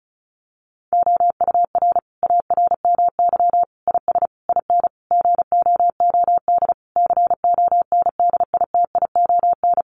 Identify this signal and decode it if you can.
Morse code, heard as a keyed tone, about 34 wpm, 700 Hz. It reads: OUR ARMY IS IN GOOD CONDITION
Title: Sentences from Top 1500 Words 35wpm